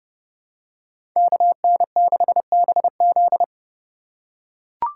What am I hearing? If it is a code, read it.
Morse code, heard as a keyed tone, about 29 wpm, 700 Hz. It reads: KN6BZ